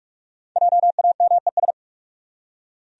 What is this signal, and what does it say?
Morse code, heard as a keyed tone, about 45 wpm, 700 Hz. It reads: JAMES
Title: USA Common Names - Max 5 Letters - 45wpm